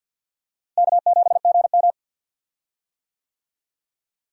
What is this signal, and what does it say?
Morse code, heard as a keyed tone, about 49 wpm, 700 Hz. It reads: K7GM
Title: USA Call Signs - 4 Characters - Repeated 2x - 50wpm